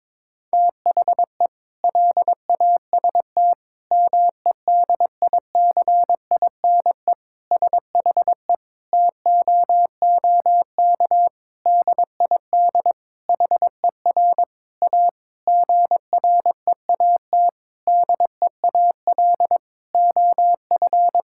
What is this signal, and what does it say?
Morse code, heard as a keyed tone, about 22 wpm, 700 Hz. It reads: THE LAST MEDICINE SHE TOOK DID HER A GREAT DEAL OF
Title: Sentences from Top 1000 Words - Review Part 6 - 22wpm